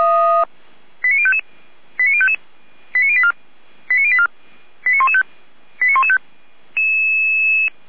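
Five-tone selective call (ZVEI), heard as quick runs of five tones, tones of about 70 ms. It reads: 89055 89055 89084 89084 89115 89115
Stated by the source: ZVEI five-tone selective call 89120 received by SDR.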